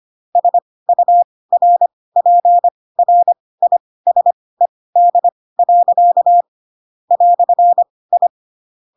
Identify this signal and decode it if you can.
Morse code, heard as a keyed tone, about 25 wpm, spaced as for 18 wpm, 700 Hz. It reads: SURPRISED. "I
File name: Pinocchio / Pinocchio ch7 12 25 ik1hge